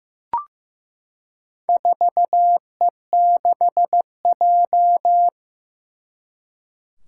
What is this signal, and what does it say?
Morse code, heard as a keyed tone, about 15 wpm, 700 Hz. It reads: 4E6J
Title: Four Challenging Letter-Number Combinations 15wpm